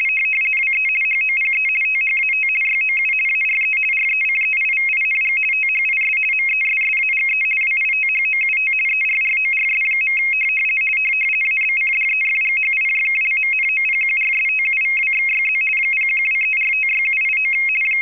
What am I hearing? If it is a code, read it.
RTTY, about 50 baud. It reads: WE HOPE THAT YOU ARE RECEIVING THIS MESSAGE WITHOUT ERRORS. ABCDEFGHIJKLMNOPQRSTUVWXYZ 0123456789
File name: rtty425 / rtty50.wav / rtty50